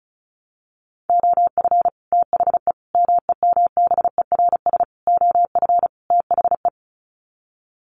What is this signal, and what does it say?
Morse code, heard as a keyed tone, about 34 wpm, 700 Hz. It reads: OF THE MEMBERS OF THE
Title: Sets of 5 Words Encore2 35wpm